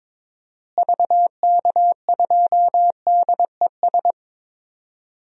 Morse code, heard as a keyed tone, about 22 wpm, 700 Hz. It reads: VK2DES
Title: DXCC-100 Call Signs - 6 Characters - Repeated 2x - 22wpm